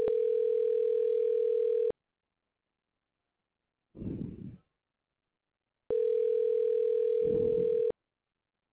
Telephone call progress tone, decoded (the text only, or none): ringback tone